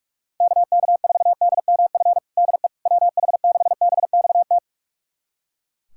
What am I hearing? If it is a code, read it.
Morse code, heard as a keyed tone, about 42 wpm, 700 Hz. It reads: KK4DMF DE WH6BXT
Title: USA Callsign DE Callsign 45wpm